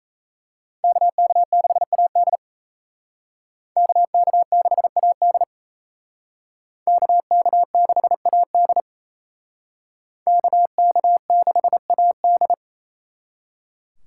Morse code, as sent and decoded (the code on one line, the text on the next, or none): -.- -.- -.... .- -.. / -.- -.- -.... .- -.. / -.- -.- -.... .- -.. / -.- -.- -.... .- -..
KK6AD KK6AD KK6AD KK6AD